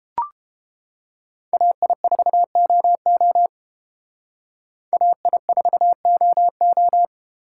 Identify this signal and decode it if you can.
Morse code, heard as a keyed tone, about 29 wpm, 700 Hz. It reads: AI4OO AI4OO